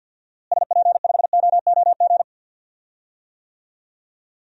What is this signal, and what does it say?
Morse code, heard as a keyed tone, about 52 wpm, 700 Hz. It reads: SP5OOG
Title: DXCC-100 Call Signs - 6 Characters - Repeated 3x - 50wpm